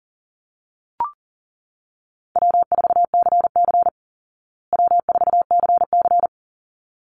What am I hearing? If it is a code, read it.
Morse code, heard as a keyed tone, about 40 wpm, 700 Hz. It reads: W4CC W4CC
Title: Easy Call Signs Repeated 2x - 40wpm